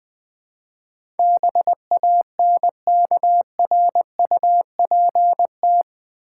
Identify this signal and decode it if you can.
Morse code, heard as a keyed tone, about 21 wpm, 700 Hz. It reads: BANKRUPT
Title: Build A Word - 4 Letters At A Time - 10-Letter Words - 20wpm